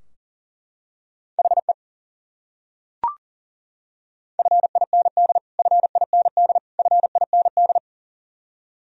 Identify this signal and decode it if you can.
Morse code, heard as a keyed tone, about 40 wpm, 700 Hz. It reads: HE FIND FIND FIND
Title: Top 100 Words Repeated 40wpm